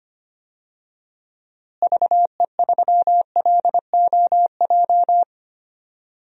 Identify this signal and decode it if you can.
Morse code, heard as a keyed tone, about 25 wpm, 700 Hz. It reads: VE3LOJ
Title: DXCC-100 Call Signs - 6 Characters - Repeated 3x - 25wpm